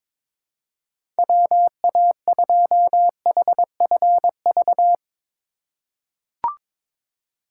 Morse code, heard as a keyed tone, about 22 wpm, 700 Hz. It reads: WA2HFV
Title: USA Call Signs - 2x Character Spacing - 22wpm